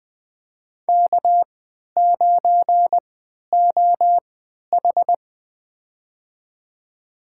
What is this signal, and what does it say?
Morse code, heard as a keyed tone, about 20 wpm, spaced as for 11 wpm, 700 Hz. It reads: K9OH
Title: USA Call Signs - 4 Characters - 3x Character Spacing - 20wpm